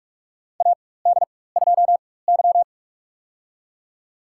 Morse code, heard as a keyed tone, about 46 wpm, spaced as for 21 wpm, 700 Hz. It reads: AD2Y
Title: USA Call Signs - 4x Character Spacing - 45wpm